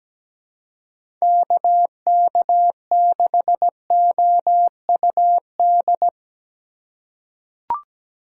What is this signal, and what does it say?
Morse code, heard as a keyed tone, about 17 wpm, 700 Hz. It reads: KK6OUD
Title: USA Call Signs - 6 Characters - 3x Character Spacing - 17wpm